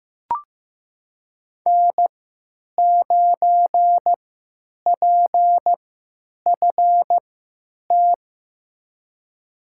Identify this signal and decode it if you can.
Morse code, heard as a keyed tone, about 15 wpm, spaced as for 9 wpm, 700 Hz. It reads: N9PFT
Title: USA Call Signs - 5 Characters - 3x Character Spacing - 15wpm